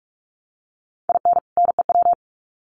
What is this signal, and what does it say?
Morse code, heard as a keyed tone, about 44 wpm, 700 Hz. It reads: IN NEW
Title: Sets of 2 Words 45wpm